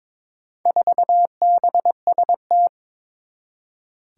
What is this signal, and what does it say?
Morse code, heard as a keyed tone, about 22 wpm, 700 Hz. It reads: 4BST
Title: Four Challenging Letter-Number Combinations 22wpm